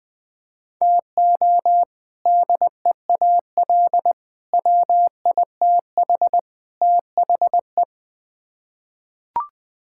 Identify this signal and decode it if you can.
Morse code, heard as a keyed tone, about 21 wpm, 700 Hz. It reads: TO DEAL WITH THE
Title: Build A Phrase - 4-Word Phrases - 20wpm